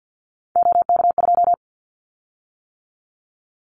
Morse code, heard as a keyed tone, about 49 wpm, 700 Hz. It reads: OK2
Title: Callsign Prefixes for Top 25 DXCC - 50wpm